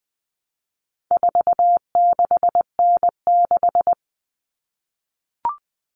Morse code, heard as a keyed tone, about 20 wpm, 700 Hz. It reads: 46N6
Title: Common Cut Numbers - 4-Digit 20wpm